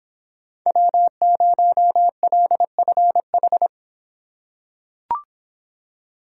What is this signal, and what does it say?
Morse code, heard as a keyed tone, about 26 wpm, 700 Hz. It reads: W0LFH